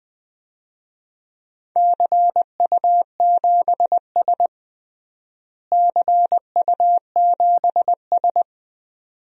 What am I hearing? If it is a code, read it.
Morse code, heard as a keyed tone, about 20 wpm, 700 Hz. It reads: CU7S CU7S
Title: DXCC-100 Call Signs - 4 Characters - Repeated 2x - 20wpm